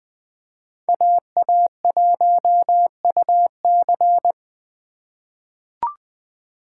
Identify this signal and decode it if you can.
Morse code, heard as a keyed tone, about 20 wpm, 700 Hz. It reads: AA1UC
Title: K1USN SST QSO Practice - 20wpm